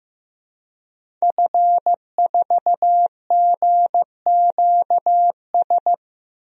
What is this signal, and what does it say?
Morse code, heard as a keyed tone, about 16 wpm, 700 Hz. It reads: F4GQS
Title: DXCC-100 Call Signs - 5 Characters - 15wpm